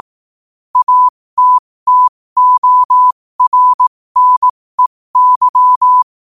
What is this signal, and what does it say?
Morse code, heard as a keyed tone, about 18 wpm, spaced as for 15 wpm, 1000 Hz. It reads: ATTORNEY
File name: qotdfiles / 15wpm